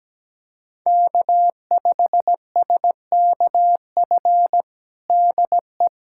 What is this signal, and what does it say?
Morse code, heard as a keyed tone, about 17 wpm, 700 Hz. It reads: K5SKF DE
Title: USA Callsign DE Callsign - 5 Characters - 17wpm